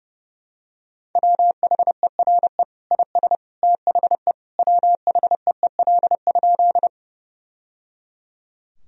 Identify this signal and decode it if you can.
Morse code, heard as a keyed tone, about 30 wpm, 700 Hz. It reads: WHERE IS THE WHEEL?
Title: Sentences from Top 1000 Words - Review Part 3 - 30wpm